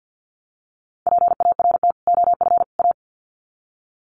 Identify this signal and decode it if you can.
Morse code, heard as a keyed tone, about 50 wpm, 700 Hz. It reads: PART OF A
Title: Sets of 3 Words 50wpm